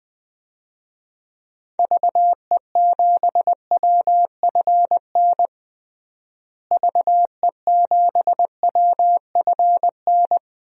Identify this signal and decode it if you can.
Morse code, heard as a keyed tone, about 20 wpm, 700 Hz. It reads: VE7WFN VE7WFN
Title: DXCC-100 Call Signs - 6 Characters - Repeated 2x - 20wpm